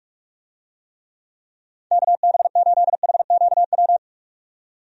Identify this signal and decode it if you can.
Morse code, heard as a keyed tone, about 46 wpm, 700 Hz. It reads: KB8HQW
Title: USA Call Signs - 6 Characters - 45wpm